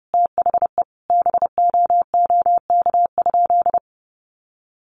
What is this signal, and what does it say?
Morse code, heard as a keyed tone, about 30 wpm, 700 Hz. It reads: THE BOOK?
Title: Sentences from Top 1200 Words 30wpm